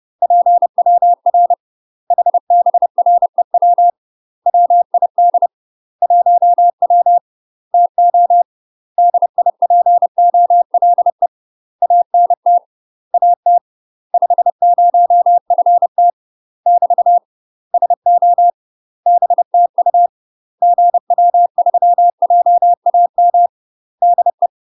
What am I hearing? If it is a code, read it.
Morse code, heard as a keyed tone, about 30 wpm, 700 Hz. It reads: PWR HBREW WID 1W TO DIPOLE ANT AT 50FT <BT> SO BTU GW3JAM DE
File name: UK MASTER QSOs TRAINING / 30WPM / UK MASTER QSOs 0019